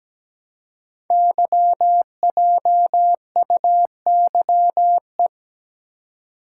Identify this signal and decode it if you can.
Morse code, heard as a keyed tone, about 17 wpm, 700 Hz. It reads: YJUYE
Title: Code Groups - 5 Letters - 17wpm